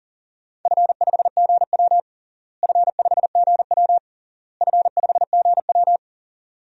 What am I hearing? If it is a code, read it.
Morse code, heard as a keyed tone, about 37 wpm, 700 Hz. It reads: F5GW F5GW F5GW
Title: DXCC-100 Call Signs - 4 Characters - Repeated 3x - 40wpm